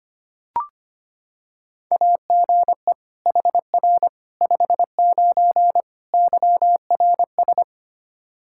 Morse code, heard as a keyed tone, about 25 wpm, 700 Hz. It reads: AGE HR 59 YRS
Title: Practice QSO - Exchange 3 - More Info - 25wpm